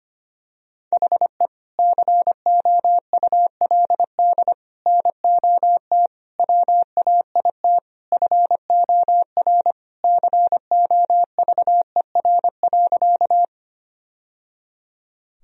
Morse code, heard as a keyed tone, about 25 wpm, 700 Hz. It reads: HE COULD NOT WAIT FOR COVER.
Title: Sentences from Top 1000 Words - Review Part 1 - 25wpm